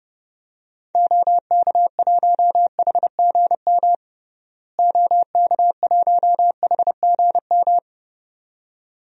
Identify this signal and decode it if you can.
Morse code, heard as a keyed tone, about 32 wpm, 700 Hz. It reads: OK1HGM OK1HGM
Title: DXCC-100 Call Signs - 6 Characters - Repeated 2x - 30wpm